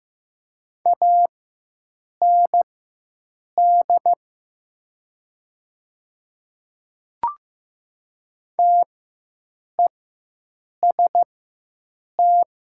Morse code, heard as a keyed tone, about 15 wpm, spaced as for 7 wpm, 700 Hz. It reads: AND TEST
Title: 4x Character Spacing - Top 500 Words - 15wpm